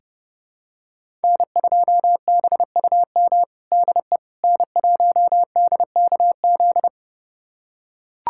Morse code, heard as a keyed tone, about 30 wpm, 700 Hz. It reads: N2BUM DE N1DKZ
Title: USA Callsign DE Callsign - 5 Characters - 30wpm